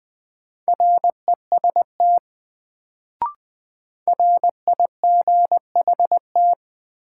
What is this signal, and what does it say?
Morse code, heard as a keyed tone, about 20 wpm, 700 Hz. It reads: REST RIGHT
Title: Sets of 2 Words - New Word Drill - 20wpm